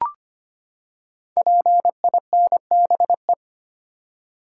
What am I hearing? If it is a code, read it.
Morse code, heard as a keyed tone, about 25 wpm, 700 Hz. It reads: PINBE
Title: Code Groups - 5 Letter:Numbers - 25wpm